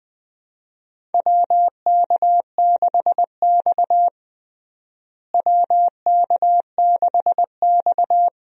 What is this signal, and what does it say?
Morse code, heard as a keyed tone, about 21 wpm, 700 Hz. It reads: WK6X WK6X
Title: USA Call Signs - 4 Characters - Repeated 2x - 20wpm